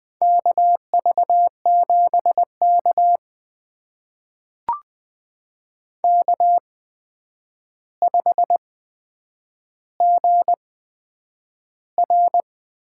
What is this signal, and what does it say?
Morse code, heard as a keyed tone, about 20 wpm, spaced as for 9 wpm, 700 Hz. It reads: KV7K K5GR
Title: USA Call Signs - 4 Characters - 8x Character Spacing - 20wpm